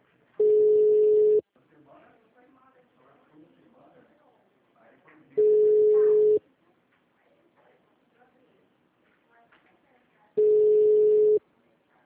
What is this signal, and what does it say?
Ringback tone (Europe, 425 Hz), with speech.
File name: out- -2020- - - .6518